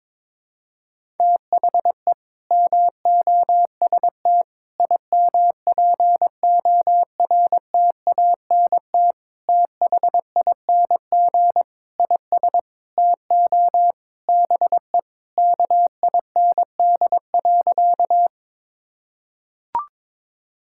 Morse code, heard as a keyed tone, about 22 wpm, 700 Hz. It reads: THE MOST IMPORTANT THING IS TO BE KIND.
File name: practice / 22wpm / Sentences with 4-Word Phrases - 22wpm